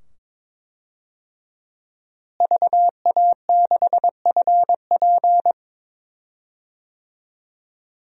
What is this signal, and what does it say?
Morse code, heard as a keyed tone, about 22 wpm, 700 Hz. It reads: VA6FP
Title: DXCC-100 Call Signs - 5 Characters - Repeated 2x - 22wpm